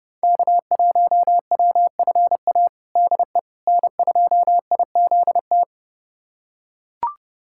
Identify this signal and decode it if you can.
Morse code, heard as a keyed tone, about 30 wpm, 700 Hz. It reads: K1WFA DE N2IZT
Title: USA Callsign DE Callsign - 5 Characters - 30wpm